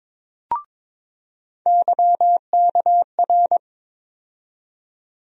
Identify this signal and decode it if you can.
Morse code, heard as a keyed tone, about 22 wpm, 700 Hz. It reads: YKR
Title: USA Call Sign Postfixes - 3 Letters 22wpm